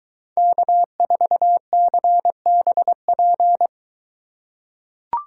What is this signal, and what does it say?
Morse code, heard as a keyed tone, about 21 wpm, 700 Hz. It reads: K4CBP